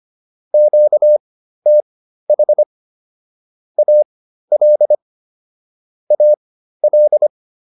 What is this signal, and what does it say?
Morse code, heard as a keyed tone, about 25 wpm, spaced as for 13 wpm, 600 Hz. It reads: QTH AL AL